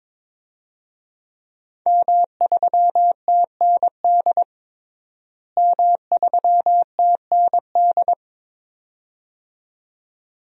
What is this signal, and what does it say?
Morse code, heard as a keyed tone, about 22 wpm, 700 Hz. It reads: M3TND M3TND
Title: DXCC-100 Call Signs - 5 Characters - Repeated 2x - 22wpm